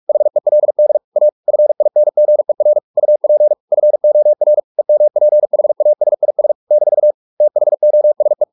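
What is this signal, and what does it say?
Morse code, heard as a keyed tone, about 44 wpm, 600 Hz. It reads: HELD A FINGER UP FOR EMPHASIS - THOSE
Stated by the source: Morse practice at 45 wpm, the novel Never by Ken Follett in English, from CW med Gnister.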